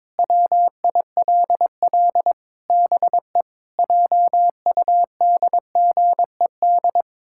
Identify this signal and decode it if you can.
Morse code, heard as a keyed tone, about 22 wpm, 700 Hz. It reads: WILL BE JUDGED
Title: Pangram Challenge v2 - 22wpm